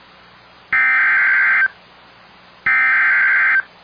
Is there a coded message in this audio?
Yes: Emergency Alert System (SAME) tones (header bursts) from 0.7 to 3.6 s, heard as harsh buzzing bursts.